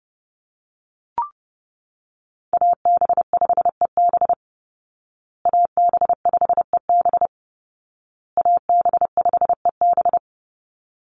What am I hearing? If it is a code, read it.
Morse code, heard as a keyed tone, about 30 wpm, 700 Hz. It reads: AB5EB AB5EB AB5EB
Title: Call Signs Repeated 2x - 30wpm